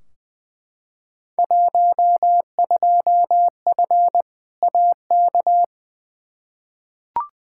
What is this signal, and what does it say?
Morse code, heard as a keyed tone, about 20 wpm, 700 Hz. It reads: 12F AK
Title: Field Day Exchanges Encore - 20wpm